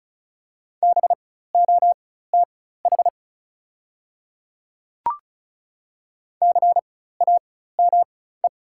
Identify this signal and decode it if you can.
Morse code, heard as a keyed tone, about 34 wpm, spaced as for 16 wpm, 700 Hz. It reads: BOTH CAME
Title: 4x Character Spacing - Top 300 Words - 35wpm